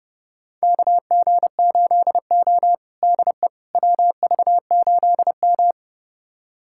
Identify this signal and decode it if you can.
Morse code, heard as a keyed tone, about 30 wpm, 700 Hz. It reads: KG8O DE WV8M
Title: USA Callsign DE Callsign - 4 Characters - 30wpm